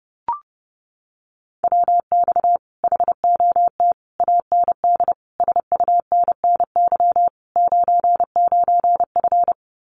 Morse code, heard as a keyed tone, about 30 wpm, 700 Hz. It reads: WX HOT AND SUNNY 99F
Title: Common QSO Phrases Encore 30wpm